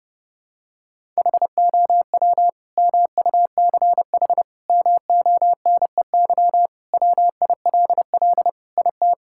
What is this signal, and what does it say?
Morse code, heard as a keyed tone, about 30 wpm, 700 Hz. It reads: HOW MUCH MONEY WILL IT
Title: Sentences from Top 1000 Words - Review Part 5 - 30wpm